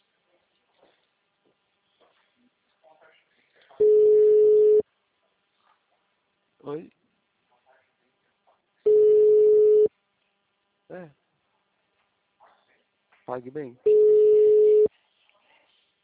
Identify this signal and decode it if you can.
Ringback tone (Europe, 425 Hz), with speech.
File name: out- -2011- - - .2020